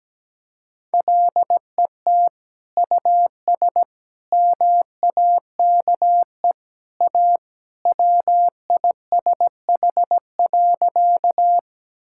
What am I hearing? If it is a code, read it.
Morse code, heard as a keyed tone, about 17 wpm, 700 Hz. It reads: LET US MAKE A WISH.
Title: Sentences from Top 1600 Words 17wpm